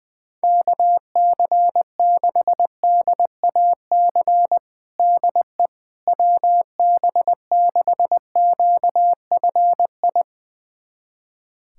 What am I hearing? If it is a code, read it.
Morse code, heard as a keyed tone, about 21 wpm, 700 Hz. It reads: KC6DAC DE WB6QFI
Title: USA Callsign DE Callsign - 6 Characters - 20wpm